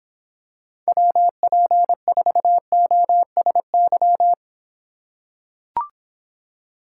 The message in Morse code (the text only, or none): WP4OSY